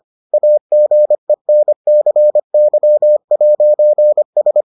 Morse code, heard as a keyed tone, about 25 wpm, 600 Hz. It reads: AGENCY'S